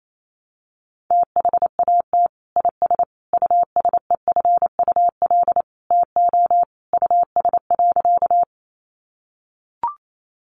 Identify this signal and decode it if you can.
Morse code, heard as a keyed tone, about 28 wpm, 700 Hz. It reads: THAT IS USEFUL TO US.
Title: Sentences from Top 700 Words 28wpm